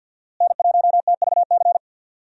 Morse code, heard as a keyed tone, about 46 wpm, 700 Hz. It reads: N1TVC
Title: USA Call Signs - 5 Characters - Repeated 2x - 50wpm